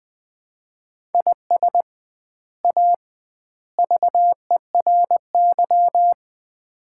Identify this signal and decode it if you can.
Morse code, heard as a keyed tone, about 20 wpm, 700 Hz. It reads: IS A VERY
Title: 2x Word-Spacing - Sets of 3 Words - 20wpm